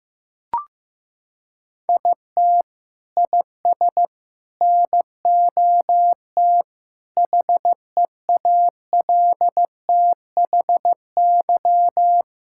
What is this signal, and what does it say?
Morse code, heard as a keyed tone, about 15 wpm, 700 Hz. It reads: IT IS NOT HEALTHY
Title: Sentences from Top 1000 Words - Review Part 6 - 15wpm